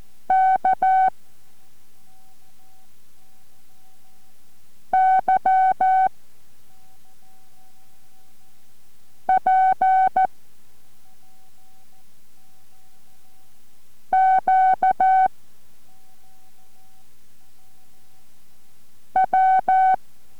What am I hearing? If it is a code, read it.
Morse code, heard as a keyed tone, about 14 wpm, 760 Hz. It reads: K Y P Q W